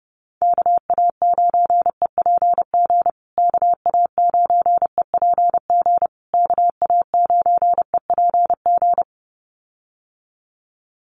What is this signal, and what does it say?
Morse code, heard as a keyed tone, about 30 wpm, 700 Hz. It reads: KA9EPG KA9EPG KA9EPG
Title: USA Call Signs Repeated 3x 30wpm